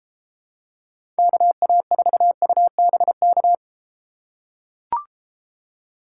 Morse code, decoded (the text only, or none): KA4UBK